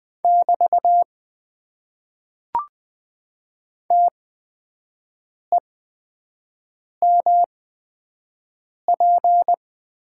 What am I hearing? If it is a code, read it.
Morse code, heard as a keyed tone, about 20 wpm, spaced as for 5 wpm, 700 Hz. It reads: <BT> TEMP
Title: Top 90 QSO Elements - 8x Character Spacing - 20wpm